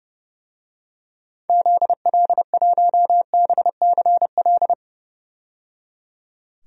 Morse code, heard as a keyed tone, about 31 wpm, 700 Hz. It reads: ZL1BCL
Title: DXCC-100 Call Signs - 6 Characters - 30wpm